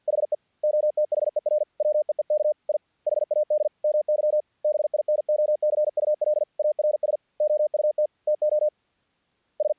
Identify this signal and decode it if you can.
Morse code, heard as a keyed tone, about 48 wpm, 600 Hz. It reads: HE OTHER WEEK I HAD MY BINOCULARS OUT TO S
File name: archived / CRSnet